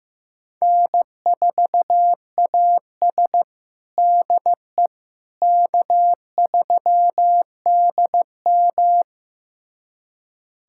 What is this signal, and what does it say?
Morse code, heard as a keyed tone, about 15 wpm, 700 Hz. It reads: N4AS DE K3DM
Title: USA Callsign DE Callsign - 4 Characters - 15wpm